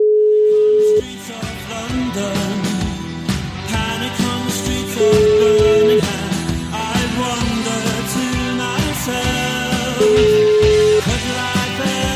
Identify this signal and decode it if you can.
Ringback tone (Europe, 425 Hz), with noise.